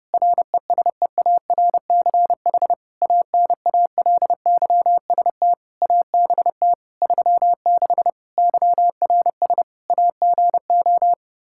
Morse code, heard as a keyed tone, about 30 wpm, 700 Hz. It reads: RESEARCH ANALYST ABT 36 YRS AGO
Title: Practice QSO - Exchange 3 - More Info - 30wpm